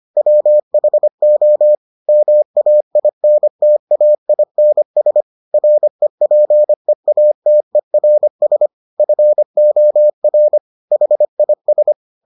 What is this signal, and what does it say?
Morse code, heard as a keyed tone, about 25 wpm, 600 Hz. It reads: WHO MAINTAINS REPEATERS FOR HIS